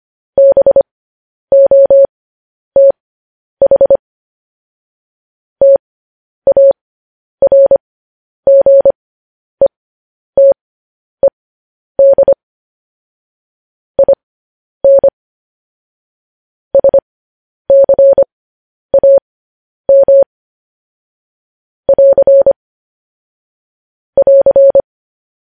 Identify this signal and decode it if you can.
Morse code, heard as a keyed tone, about 25 wpm, spaced as for 10 wpm, 550 Hz. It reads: BOTH TARGETED IN SCAM <AR> <AR>
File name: podcast / News headlines-1x-25x10